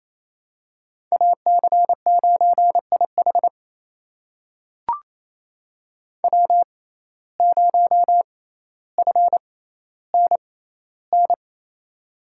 Morse code, heard as a keyed tone, about 28 wpm, spaced as for 14 wpm, 700 Hz. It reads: AC9IH W0FNN
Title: USA Call Signs - 5 Characters - 6x Character Spacing - 28wpm